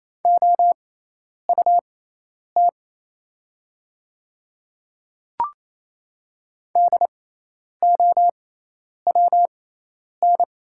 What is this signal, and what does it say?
Morse code, heard as a keyed tone, about 27 wpm, spaced as for 10 wpm, 700 Hz. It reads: OUT DOWN
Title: 6x Character Spacing - Top 100 Words - 28wpm